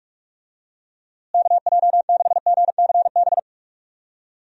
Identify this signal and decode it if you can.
Morse code, heard as a keyed tone, about 47 wpm, 700 Hz. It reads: KJ6GCB
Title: USA Call Signs - 6 Characters - Repeated 2x - 45wpm